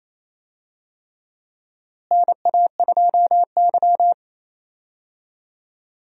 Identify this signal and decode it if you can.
Morse code, heard as a keyed tone, about 28 wpm, 700 Hz. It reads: NA2Y
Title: USA Call Signs - 4 Characters - Repeated 3x - 28wpm